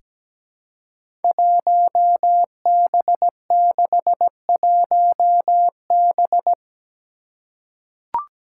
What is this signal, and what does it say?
Morse code, heard as a keyed tone, about 17 wpm, 700 Hz. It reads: 1B61B
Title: Five Challenging Letter-Number Combinations 17wpm